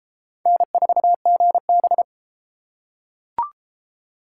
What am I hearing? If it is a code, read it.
Morse code, heard as a keyed tone, about 30 wpm, 700 Hz. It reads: N4GB